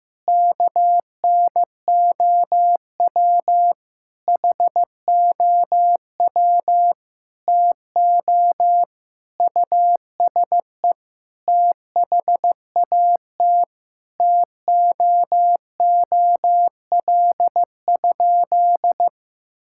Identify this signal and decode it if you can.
Morse code, heard as a keyed tone, about 15 wpm, 700 Hz. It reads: KNOW HOW TO USE THAT TOOL?
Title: Sentences from Top 1000 Words - Review Part 6 - 15wpm